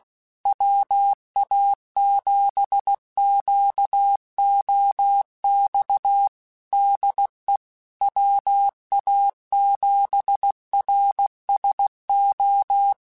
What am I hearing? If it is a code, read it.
Morse code, heard as a keyed tone, about 16 wpm, 800 Hz. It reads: WA7QOX DE WA7RSO